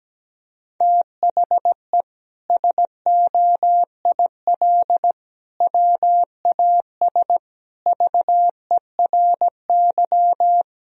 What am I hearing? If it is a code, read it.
Morse code, heard as a keyed tone, about 17 wpm, 700 Hz. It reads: THE SOIL WAS VERY
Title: Sentences from Top 1000 Words - Review Part 4 - 17wpm